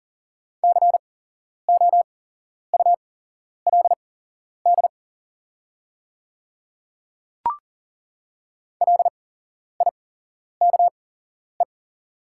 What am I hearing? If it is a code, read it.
Morse code, heard as a keyed tone, about 40 wpm, spaced as for 11 wpm, 700 Hz. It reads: COULD LIKE
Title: 8x Character Spacing - Top 100 Words - 40wpm